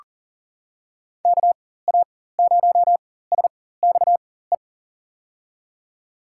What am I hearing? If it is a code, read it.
Morse code, heard as a keyed tone, about 38 wpm, spaced as for 18 wpm, 700 Hz. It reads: KA0SXE
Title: USA Call Signs - 6 Characters - 4x Character Spacing - 40wpm